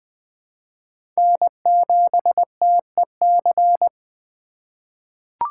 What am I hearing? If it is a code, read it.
Morse code, heard as a keyed tone, about 20 wpm, 700 Hz. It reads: N7TEC
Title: USA Call Signs - 5 Characters - 3x Character Spacing - 20wpm